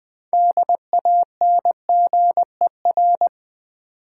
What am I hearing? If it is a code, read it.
Morse code, heard as a keyed tone, about 20 wpm, 700 Hz. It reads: DANGER
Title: Build A Word - 3 Letters At A Time - 6-Letter Words - 20wpm